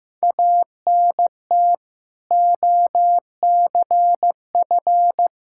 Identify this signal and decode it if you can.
Morse code, heard as a keyed tone, about 15 wpm, 700 Hz. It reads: ANT OCF
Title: Practice QSO - Exchange 2 - Info - 15wpm